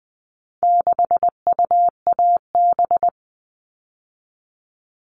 Morse code, heard as a keyed tone, about 20 wpm, 700 Hz. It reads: 6UAB
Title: All Cut Numbers - 4-Digit 20wpm